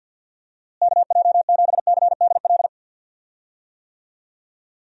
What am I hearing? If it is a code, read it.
Morse code, heard as a keyed tone, about 52 wpm, 700 Hz. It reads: KJ7CDL
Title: USA Call Signs - 6 Characters - Repeated 3x - 50wpm